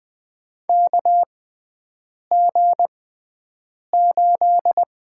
Morse code, heard as a keyed tone, about 20 wpm, spaced as for 7 wpm, 700 Hz. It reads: KG8
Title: USA Call Signs - 5 Characters - 6x Character Spacing - 20wpm